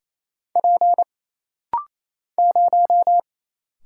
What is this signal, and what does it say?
Morse code, heard as a keyed tone, about 27 wpm, 700 Hz. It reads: P 0
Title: Single Letter-Number - Rapid-Fire 28wpm